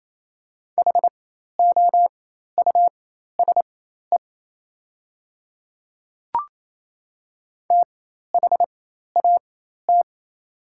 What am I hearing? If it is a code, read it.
Morse code, heard as a keyed tone, about 28 wpm, spaced as for 13 wpm, 700 Hz. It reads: HOUSE THAT
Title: 4x Character Spacing - Top 500 Words - 28wpm